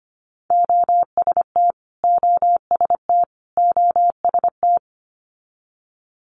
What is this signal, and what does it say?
Morse code, heard as a keyed tone, about 25 wpm, 700 Hz. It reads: OST OST OST
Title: Multi-letter Phonemes Repeated 3x 25wpm